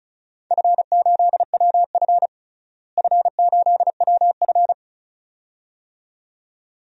Morse code, heard as a keyed tone, about 36 wpm, 700 Hz. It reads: F8WF F8WF
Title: DXCC-100 Call Signs - 4 Characters - Repeated 2x - 35wpm